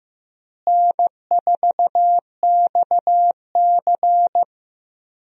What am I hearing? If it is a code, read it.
Morse code, heard as a keyed tone, about 15 wpm, 700 Hz. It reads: N4XC
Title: USA Call Signs - 4 Characters - Repeated 3x - 15wpm